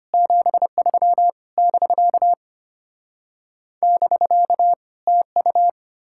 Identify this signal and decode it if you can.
Morse code, heard as a keyed tone, about 33 wpm, 700 Hz. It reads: 73 <BK> <BK> TU